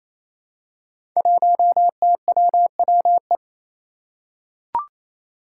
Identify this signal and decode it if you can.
Morse code, heard as a keyed tone, about 28 wpm, 700 Hz. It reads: 1TWWE
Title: Five Challenging Letter-Number Combinations 28wpm